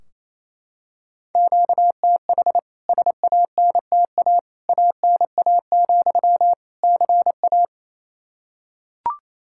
Morse code, heard as a keyed tone, about 28 wpm, 700 Hz. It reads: QTH SANTA ANA, CA
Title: K1USN SST QSO Practice 28wpm